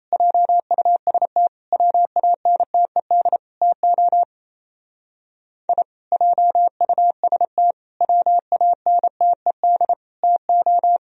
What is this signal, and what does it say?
Morse code, heard as a keyed tone, about 35 wpm, 700 Hz. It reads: JUST WANTED TO I JUST WANTED TO